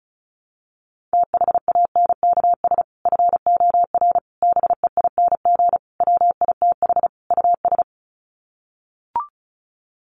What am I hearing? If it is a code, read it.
Morse code, heard as a keyed tone, about 34 wpm, 700 Hz. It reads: THANKS FOR BEING WITH US
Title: Sets of 5 Words 35wpm